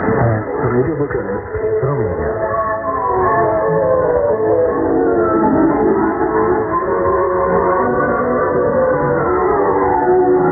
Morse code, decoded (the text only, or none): none